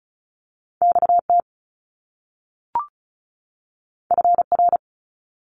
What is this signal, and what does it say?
Morse code, heard as a keyed tone, about 36 wpm, 700 Hz. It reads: XT FR
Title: Top Two Letter Combinations in Words 35wpm